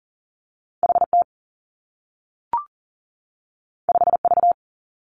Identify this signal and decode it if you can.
Morse code, heard as a keyed tone, about 40 wpm, 700 Hz. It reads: H T 5 V
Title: Two Challenging Letter-Number Combinations 40wpm